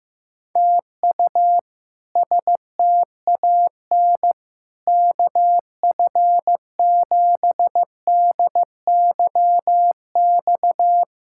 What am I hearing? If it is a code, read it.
Morse code, heard as a keyed tone, about 16 wpm, 700 Hz. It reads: TU STAN KF7DYX
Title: K1USN SST QSO Practice - 15wpm